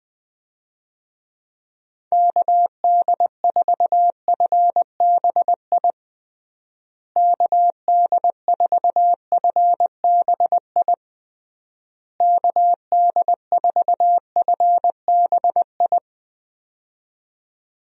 Morse code, heard as a keyed tone, about 20 wpm, 700 Hz. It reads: KD4FBI KD4FBI KD4FBI
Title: USA Call Signs - 6 Characters - Repeated 3x - 20wpm